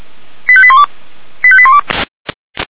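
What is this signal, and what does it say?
Five-tone selective call (ZVEI), heard as quick runs of five tones, tones of about 70 ms. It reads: 85812 85812
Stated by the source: ZVEI five-tone call received on SDR.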